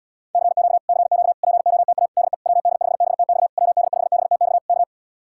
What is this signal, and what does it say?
Morse code, heard as a keyed tone, about 80 wpm, 700 Hz. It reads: CQ CQ POTA DE WD5BTX WD5BTX K